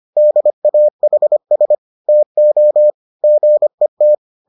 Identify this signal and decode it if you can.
Morse code, heard as a keyed tone, about 25 wpm, 600 Hz. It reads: DAHS TO GET